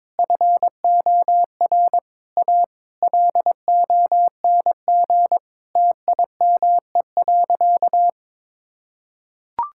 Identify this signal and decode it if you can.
Morse code, heard as a keyed tone, about 22 wpm, 700 Hz. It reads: FOR A LONG TIME.
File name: practice / 22wpm / Sentences with 3-Word Phrases - 22wpm